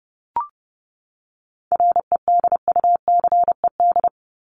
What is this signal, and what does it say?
Morse code, heard as a keyed tone, about 30 wpm, 700 Hz. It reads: REDUCED
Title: ED Suffix - Top 100 Words with Base Words 30wpm